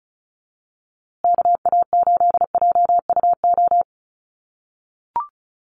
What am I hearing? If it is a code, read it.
Morse code, heard as a keyed tone, about 34 wpm, 700 Hz. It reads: KA8JUO
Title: USA Call Signs Repeated 2x 35wpm